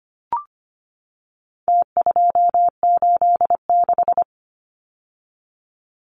Morse code, heard as a keyed tone, about 25 wpm, 700 Hz. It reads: T286
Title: Numbers - Contest Style - 4-Digit 25wpm